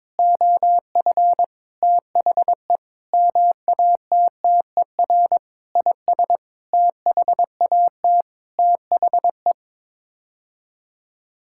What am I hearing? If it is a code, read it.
Morse code, heard as a keyed tone, about 22 wpm, 700 Hz. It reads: OF THE MATTER IS THAT THE
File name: Sets of 9 Words - 22wpm